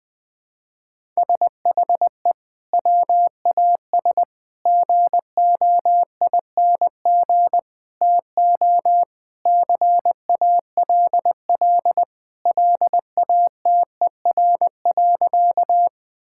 Morse code, heard as a keyed tone, about 20 wpm, 700 Hz. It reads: SHE WAS GOING TO CALL LATER.
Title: Sentences with 4-Word Phrases - 20wpm